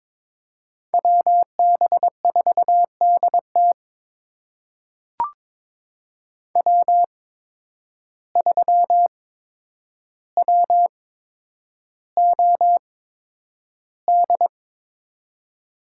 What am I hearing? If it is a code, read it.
Morse code, heard as a keyed tone, about 22 wpm, spaced as for 9 wpm, 700 Hz. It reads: WB4DT W3WOD
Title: USA Call Signs - 5 Characters - 8x Character Spacing - 22wpm